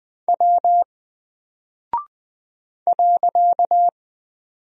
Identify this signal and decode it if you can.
Morse code, heard as a keyed tone, about 20 wpm, 700 Hz. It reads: W .
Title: Single Character (Letter:Number:Punctuation:Prosign) Alt 20wpm